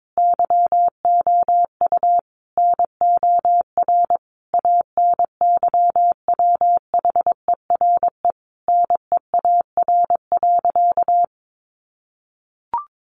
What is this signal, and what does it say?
Morse code, heard as a keyed tone, about 22 wpm, 700 Hz. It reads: YOU NOR ANYWHERE NEAR.
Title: Sentences from Top 1500 Words 22wpm